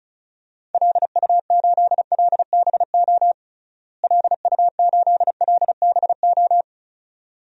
Morse code, heard as a keyed tone, about 36 wpm, 700 Hz. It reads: LU8LBO LU8LBO
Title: DXCC-100 Call Signs - 6 Characters - Repeated 2x - 35wpm